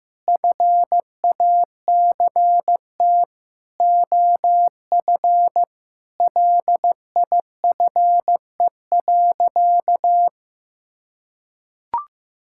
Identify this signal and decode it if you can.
Morse code, heard as a keyed tone, about 15 wpm, 700 Hz. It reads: FACT OF LIFE.
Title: Sentences from Top 1000 Words - Review Part 5 - 15wpm